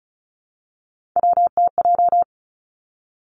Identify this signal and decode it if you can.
Morse code, heard as a keyed tone, about 36 wpm, 700 Hz. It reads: WTJ
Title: Three Challenging Letter-Number Combinations 35wpm